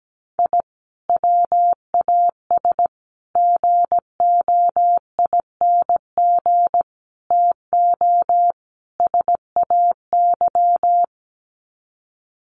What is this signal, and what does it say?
Morse code, heard as a keyed tone, about 17 wpm, 700 Hz. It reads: I WAS GOING TO SAY
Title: Sets of 5 Words 17wpm